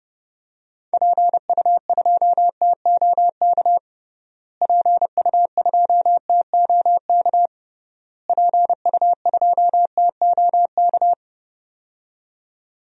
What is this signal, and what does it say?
Morse code, heard as a keyed tone, about 32 wpm, 700 Hz. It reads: PU2TOK PU2TOK PU2TOK
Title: DXCC-100 Call Signs - 6 Characters - Repeated 3x - 30wpm